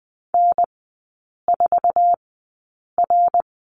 Morse code, heard as a keyed tone, about 20 wpm, spaced as for 8 wpm, 700 Hz. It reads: N4R
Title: 053 Lesson 10e - Letter r - Three Characters 20wpm